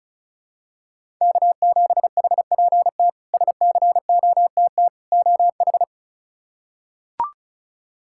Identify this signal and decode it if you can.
Morse code, heard as a keyed tone, about 34 wpm, 700 Hz. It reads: K7HPT SCOTT OH
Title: K1USN SST QSO Practice - 35wpm